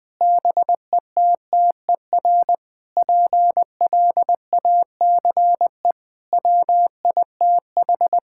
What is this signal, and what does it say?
Morse code, heard as a keyed tone, about 19 wpm, 700 Hz. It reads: BETTER PLACE WITH
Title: Sentences with 7-Word Phrases - 20wpm